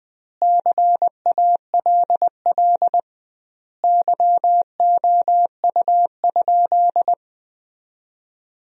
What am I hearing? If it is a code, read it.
Morse code, heard as a keyed tone, about 21 wpm, 700 Hz. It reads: CALL YOU?
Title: 2x Word-Spacing - Sentences from Top 100 Words - 20wpm